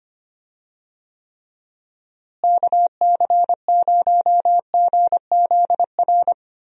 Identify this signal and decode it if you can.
Morse code, heard as a keyed tone, about 25 wpm, 700 Hz. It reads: KC0GZR
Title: USA Call Signs - 6 Characters - Repeated 2x - 25wpm